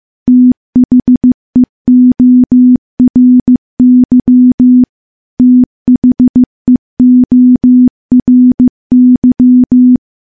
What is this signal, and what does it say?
Morse code, heard as a keyed tone, about 14 wpm, 250 Hz. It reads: THEORY THEORY
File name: morse / morse-training-15WPM